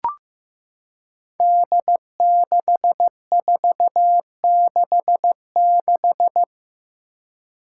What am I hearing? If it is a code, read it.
Morse code, heard as a keyed tone, about 15 wpm, 700 Hz. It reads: D6466
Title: Five Challenging Letter-Number Combinations 15wpm